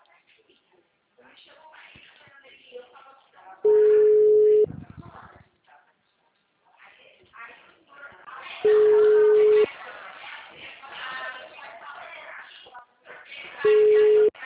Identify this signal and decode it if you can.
Ringback tone (Europe, 425 Hz), with speech.
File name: out- -2003- - - .81082